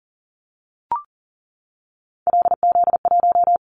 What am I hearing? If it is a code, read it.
Morse code, heard as a keyed tone, about 37 wpm, 700 Hz. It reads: LZ1
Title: Callsign Prefixes for Top 25 DXCC - 40wpm